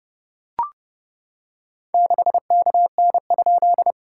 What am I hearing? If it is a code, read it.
Morse code, heard as a keyed tone, about 30 wpm, 700 Hz. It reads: 6KN?
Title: SOTA QSO Practice 30wpm